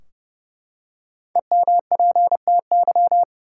Morse code, heard as a keyed tone, about 30 wpm, 700 Hz. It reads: EMPTY
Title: Five Letter Words - 30wpm